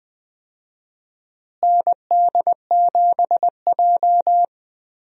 Morse code, heard as a keyed tone, about 20 wpm, 700 Hz. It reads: ND7J
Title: USA Call Signs - 4 Characters - Repeated 3x - 20wpm